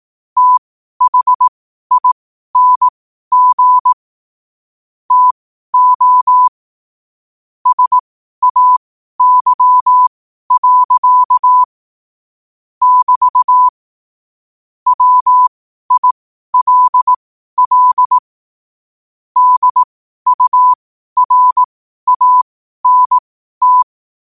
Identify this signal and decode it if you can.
Morse code, heard as a keyed tone, about 18 wpm, spaced as for 12 wpm, 1000 Hz. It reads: THING TO SAY. <BT> WILL DURANT
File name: qotdfiles / 12wpm